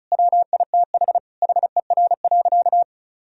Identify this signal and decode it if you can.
Morse code, heard as a keyed tone, about 36 wpm, 700 Hz. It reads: WITH HER.
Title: Sentences from Top 1000 Words - Review Part 4 - 35wpm